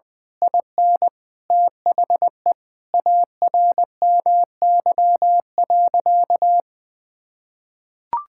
Morse code, heard as a keyed tone, about 20 wpm, 700 Hz. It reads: IN THE ARMY.
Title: Sentences from Top 1000 Words - Review Part 6 - 20wpm